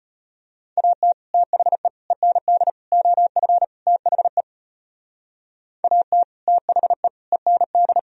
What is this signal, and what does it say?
Morse code, heard as a keyed tone, about 38 wpm, 700 Hz. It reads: AT THE END OF THE AT THE END